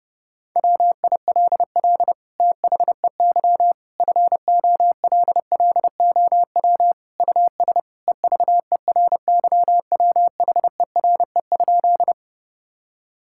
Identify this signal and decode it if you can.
Morse code, heard as a keyed tone, about 30 wpm, 700 Hz. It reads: WILL THEY FOLLOW US EVERYWHERE?
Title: Sentences from Top 1600 Words 30wpm